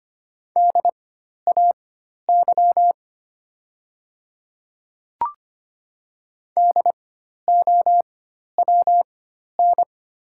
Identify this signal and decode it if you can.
Morse code, heard as a keyed tone, about 25 wpm, spaced as for 12 wpm, 700 Hz. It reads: DAY DOWN
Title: 4x Character Spacing - Top 100 Words - 25wpm